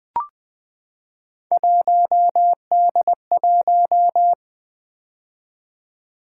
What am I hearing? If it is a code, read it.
Morse code, heard as a keyed tone, about 20 wpm, 700 Hz. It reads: 1D1
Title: Three Letter-Number Combinations - 20wpm